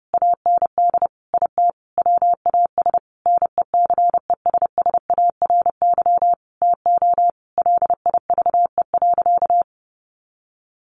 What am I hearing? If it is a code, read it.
Morse code, heard as a keyed tone, about 30 wpm, 700 Hz. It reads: AND IT WAS NECESSARY TO LIVE.
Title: Sentences from Top 800 Words 30wpm